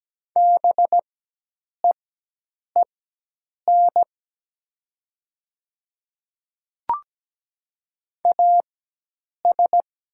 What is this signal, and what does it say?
Morse code, heard as a keyed tone, about 17 wpm, spaced as for 8 wpm, 700 Hz. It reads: BEEN AS
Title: 4x Character Spacing - Top 100 Words - 17wpm